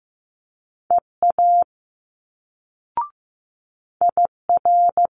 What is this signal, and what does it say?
Morse code, heard as a keyed tone, about 15 wpm, 700 Hz. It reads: EA IR
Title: Multi-letter Phonemes Quick Fire 15wpm